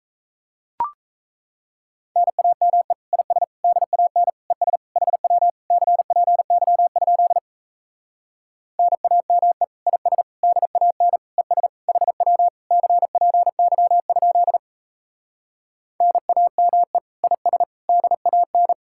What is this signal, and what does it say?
Morse code, heard as a keyed tone, about 37 wpm, 700 Hz. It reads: NAME IS DAN ES HW CPY? NAME IS DAN ES HW CPY? NAME IS DAN